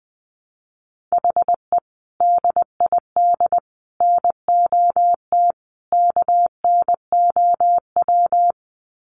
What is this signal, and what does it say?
Morse code, heard as a keyed tone, about 20 wpm, 700 Hz. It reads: HE DID NOT KNOW
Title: Sets of 4 Words 20wpm